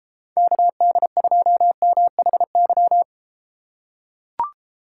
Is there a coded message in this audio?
Yes: Morse code from 0.4 to 3.0 s, heard as a keyed tone, about 32 wpm, 700 Hz. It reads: KD2MHY